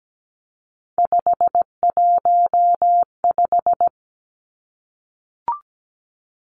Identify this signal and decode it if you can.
Morse code, heard as a keyed tone, about 17 wpm, 700 Hz. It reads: 515
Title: Three Challenging Letter-Number Combinations 17wpm